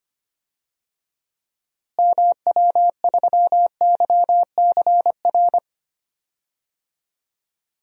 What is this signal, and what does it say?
Morse code, heard as a keyed tone, about 25 wpm, 700 Hz. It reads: MW3YCR
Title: DXCC-100 Call Signs - 6 Characters - Repeated 3x - 25wpm